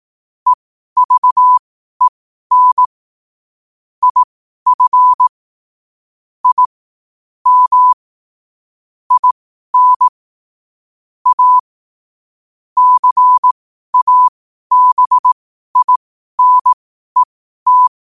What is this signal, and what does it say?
Morse code, heard as a keyed tone, about 18 wpm, spaced as for 12 wpm, 1000 Hz. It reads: EVEN IF I M IN A CABINET
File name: qotdfiles / 12wpm